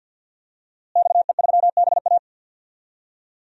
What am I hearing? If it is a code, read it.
Morse code, heard as a keyed tone, about 46 wpm, 700 Hz. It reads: XE3BA
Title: DXCC-100 Call Signs - 5 Characters - Repeated 3x - 50wpm